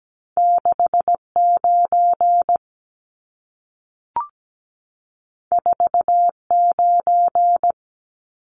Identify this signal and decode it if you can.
Morse code, heard as a keyed tone, about 17 wpm, 700 Hz. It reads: 69 49
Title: Numbers - 2-Digit 17wpm